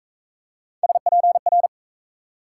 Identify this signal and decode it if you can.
Morse code, heard as a keyed tone, about 42 wpm, 700 Hz. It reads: SPR